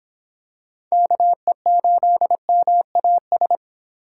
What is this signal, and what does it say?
Morse code, heard as a keyed tone, about 25 wpm, 700 Hz. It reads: KE8MAS